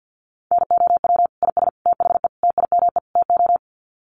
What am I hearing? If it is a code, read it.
Morse code, heard as a keyed tone, about 50 wpm, 700 Hz. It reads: NOW IS THE TIME TO
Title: Sets of 5 Words Encore2 50wpm